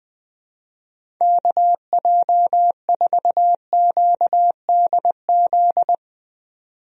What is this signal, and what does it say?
Morse code, heard as a keyed tone, about 21 wpm, 700 Hz. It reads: KJ4QDZ
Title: USA Call Signs - 6 Characters - 20wpm